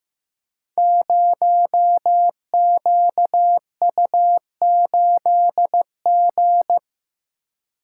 Morse code, heard as a keyed tone, about 15 wpm, 700 Hz. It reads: 0QU8G
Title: Code Groups - 5 Letter:Numbers - 15wpm